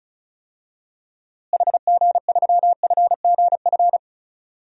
Morse code, heard as a keyed tone, about 36 wpm, 700 Hz. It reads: HG3FGF
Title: DXCC-100 Call Signs - 6 Characters - Repeated 3x - 35wpm